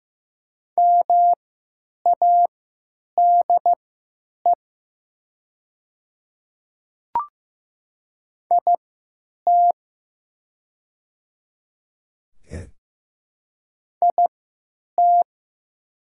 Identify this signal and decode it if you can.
Morse code, heard as a keyed tone, about 15 wpm, spaced as for 9 wpm, 700 Hz. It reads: MADE IT IT
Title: 3x Character Spacing - Top 200 Words - 15wpm